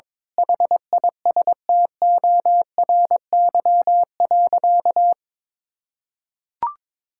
Morse code, heard as a keyed tone, about 21 wpm, 700 Hz. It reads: HISTORY.
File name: Sentences with 7-Word Phrases - 22wpm